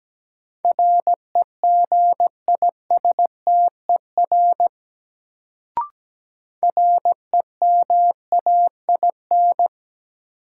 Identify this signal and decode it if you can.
Morse code, heard as a keyed tone, about 17 wpm, 700 Hz. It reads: REGISTER REMAIN
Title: Sentences from Top 1200 Words - New Word Drill Alt - 17wpm